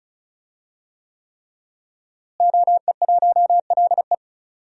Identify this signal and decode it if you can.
Morse code, heard as a keyed tone, about 36 wpm, 700 Hz. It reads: OE1LE
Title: DXCC-100 Call Signs - 5 Characters - Repeated 3x - 35wpm